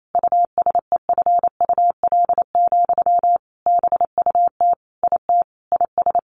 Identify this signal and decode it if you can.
Morse code, heard as a keyed tone, about 28 wpm, 700 Hz. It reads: USEFUL, BUT IT IS
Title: Sentences from Top 1400 Words 28wpm